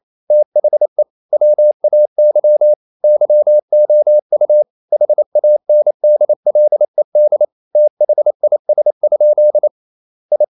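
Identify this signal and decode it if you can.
Morse code, heard as a keyed tone, about 28 wpm, 600 Hz. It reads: THE WAY YOU HANDLED THIS? I
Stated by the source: Morse code practice at 28 wpm, text of Ken Follett's novel Never in English, from CW med Gnister.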